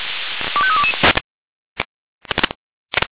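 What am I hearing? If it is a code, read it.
Five-tone selective call (ZVEI), heard as quick runs of five tones, tones of about 70 ms. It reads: 25322
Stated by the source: ZVEI five-tone selective call received by SDR.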